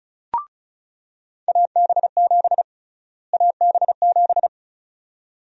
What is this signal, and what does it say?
Morse code, heard as a keyed tone, about 34 wpm, 700 Hz. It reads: AB7 AB7
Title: USA Call Sign Prefixes - Repeated 2x - 35wpm